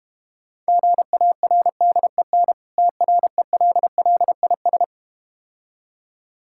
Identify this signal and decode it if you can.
Morse code, heard as a keyed tone, about 31 wpm, 700 Hz. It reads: GARDEN TRELLIS
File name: Difficult Words - Speed-Racing with Hints - 25wpm